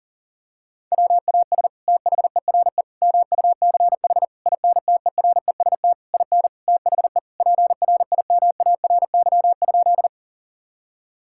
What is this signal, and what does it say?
Morse code, heard as a keyed tone, about 40 wpm, 700 Hz. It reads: WAS THERE MUCH INTEREST IN THE PRIMARY?
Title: Sentences from Top 1000 Words - Review Part 5 - 40wpm